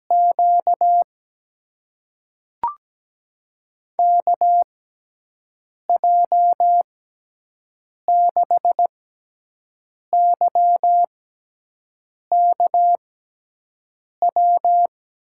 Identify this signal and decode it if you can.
Morse code, heard as a keyed tone, about 17 wpm, spaced as for 6 wpm, 700 Hz. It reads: Q KJ6YKW
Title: USA Call Signs - 6 Characters - 6x Character Spacing - 17wpm